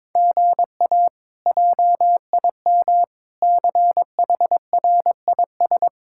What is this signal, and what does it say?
Morse code, heard as a keyed tone, about 22 wpm, 700 Hz. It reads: GA JIM CHRIS
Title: K1USN SST QSO Practice - 22wpm